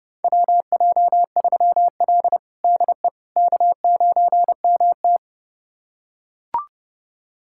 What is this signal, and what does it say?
Morse code, heard as a keyed tone, about 30 wpm, 700 Hz. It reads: WJ3L DE K9MT
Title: USA Callsign DE Callsign - 4 Characters - 30wpm